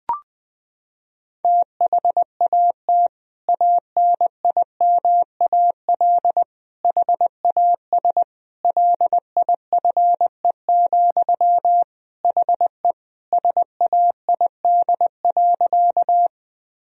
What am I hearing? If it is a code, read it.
Morse code, heard as a keyed tone, about 21 wpm, 700 Hz. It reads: THAT ANIMAL HAS LIFE, HE SAID.
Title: Sentences from Top 1000 Words - Review Part 1 - 20wpm